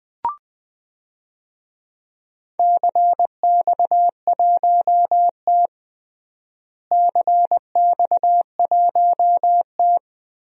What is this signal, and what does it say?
Morse code, heard as a keyed tone, about 21 wpm, 700 Hz. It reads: CX1T CX1T
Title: DXCC-100 Call Signs - 4 Characters - Repeated 2x - 20wpm